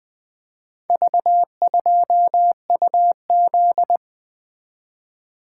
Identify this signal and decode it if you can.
Morse code, heard as a keyed tone, about 20 wpm, 700 Hz. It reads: V2UZ
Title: Four Letter-Number Combinations - 20wpm